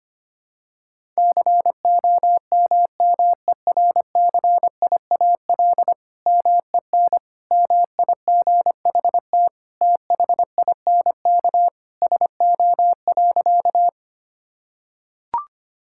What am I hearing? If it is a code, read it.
Morse code, heard as a keyed tone, about 25 wpm, 700 Hz. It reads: COMMERCIAL MEN MIGHT THINK SO.
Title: Sentences from Top 1000 Words - Review Part 5 - 25wpm